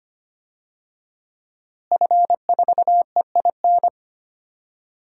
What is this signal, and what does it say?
Morse code, heard as a keyed tone, about 25 wpm, 700 Hz. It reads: F4EIN
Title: DXCC-100 Call Signs - 5 Characters - Repeated 2x - 25wpm